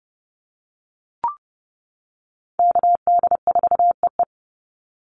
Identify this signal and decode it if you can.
Morse code, heard as a keyed tone, about 30 wpm, 700 Hz. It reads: KD4EE
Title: Call Signs - 30wpm